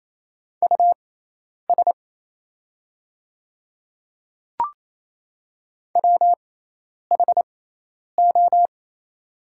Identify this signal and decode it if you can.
Morse code, heard as a keyed tone, about 28 wpm, spaced as for 10 wpm, 700 Hz. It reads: US WHO
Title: 6x Character Spacing - Top 300 Words - 28wpm